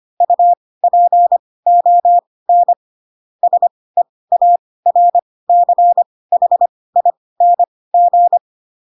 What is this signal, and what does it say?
Morse code, heard as a keyed tone, about 25 wpm, spaced as for 18 wpm, 700 Hz. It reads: UPON SEARCHING
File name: Pinocchio ch3 12 25 ik1hge